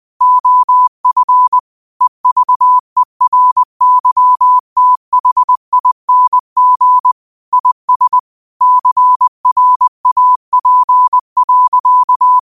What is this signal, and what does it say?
Morse code, heard as a keyed tone, about 20 wpm, 1000 Hz. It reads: OF EVERYTHING IS CRAP.